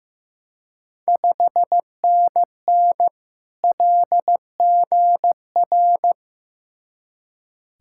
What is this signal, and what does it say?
Morse code, heard as a keyed tone, about 15 wpm, 700 Hz. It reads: 5NN LGR
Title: Ontario QSO Party Exchange 15wpm